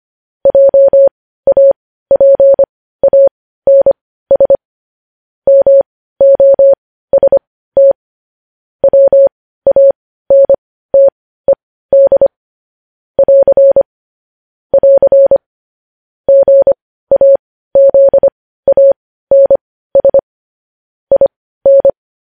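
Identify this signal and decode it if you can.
Morse code, heard as a keyed tone, about 25 wpm, spaced as for 15 wpm, 550 Hz. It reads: JAPANS MOST WANTED <AR> <AR> GAZANS IN
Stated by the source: Morse news headlines sent at 15 wpm.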